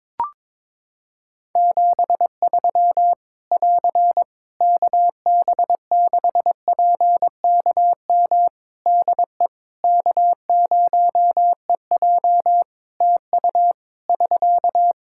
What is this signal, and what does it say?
Morse code, heard as a keyed tone, about 21 wpm, 700 Hz. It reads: 73 <AR> KB6PKM DE K0EJ TU <SK>
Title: Practice QSO - Exchange 4 - Ending - 22wpm